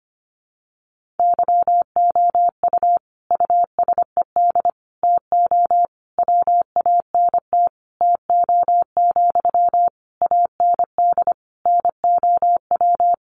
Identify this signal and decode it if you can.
Morse code, heard as a keyed tone, about 25 wpm, 700 Hz. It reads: YOU USED TO WANT TO, AND NOW
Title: Sentences from Top 600 Words 25wpm